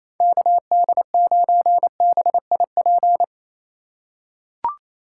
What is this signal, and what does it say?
Morse code, heard as a keyed tone, about 28 wpm, 700 Hz. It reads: KD9BIP
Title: USA Call Signs - 6 Characters - 3x Character Spacing - 28wpm